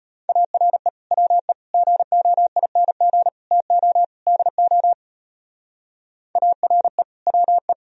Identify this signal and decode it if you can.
Morse code, heard as a keyed tone, about 29 wpm, 700 Hz. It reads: ARE WE GOING TO DO ARE WE
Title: Speed-Racing - Sets of 5 Words 25wpm